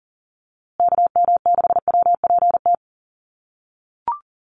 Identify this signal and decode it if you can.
Morse code, heard as a keyed tone, about 40 wpm, 700 Hz. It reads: KM6WPT
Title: USA Call Signs Repeated 3x 40wpm